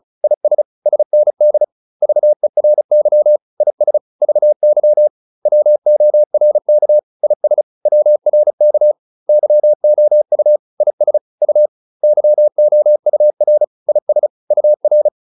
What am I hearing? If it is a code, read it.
Morse code, heard as a keyed tone, about 34 wpm, 600 Hz. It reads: IS SND VERY IS VY WORK IS WRK YOU IS U YOUR IS UR